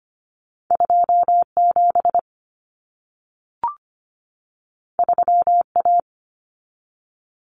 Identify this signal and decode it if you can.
Morse code, heard as a keyed tone, about 25 wpm, 700 Hz. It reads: 2 7 3 A
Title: Common Cut Numbers - 2-Digit 25wpm